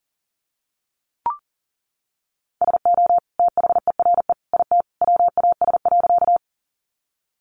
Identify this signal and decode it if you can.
Morse code, heard as a keyed tone, about 39 wpm, 700 Hz. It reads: SO THERE IT WAS.
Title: Sentences from Top 100 Words 40wpm